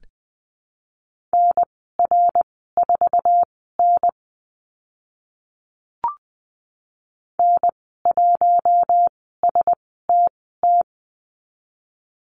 Morse code, heard as a keyed tone, about 20 wpm, spaced as for 15 wpm, 700 Hz. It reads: NR4N N1STT
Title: 055 Lesson 10g - Letter r - Call Signs 20wpm